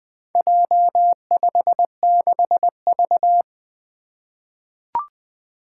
Morse code, heard as a keyed tone, about 20 wpm, 700 Hz. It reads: J56V
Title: Four Challenging Letter-Number Combinations 20wpm